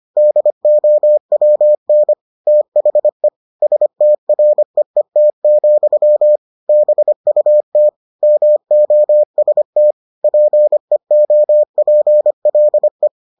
Morse code, heard as a keyed tone, about 25 wpm, 600 Hz. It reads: DOWN THE STREET, BUT MOST PEOPLE